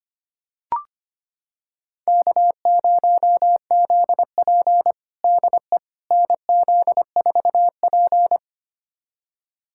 Morse code, heard as a keyed tone, about 25 wpm, 700 Hz. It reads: K0ZP DE NZ4P
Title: USA Callsign DE Callsign - 4 Characters - 25wpm